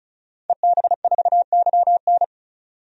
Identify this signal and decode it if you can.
Morse code, heard as a keyed tone, about 34 wpm, 700 Hz. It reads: EB4YN
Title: DXCC-100 Call Signs - 5 Characters - 35wpm